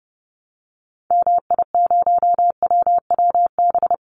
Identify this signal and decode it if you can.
Morse code, heard as a keyed tone, about 30 wpm, 700 Hz. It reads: MI0WWB
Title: Call Signs - 30wpm